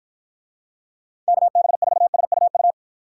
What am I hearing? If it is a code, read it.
Morse code, heard as a keyed tone, about 49 wpm, 700 Hz. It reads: KB4SUU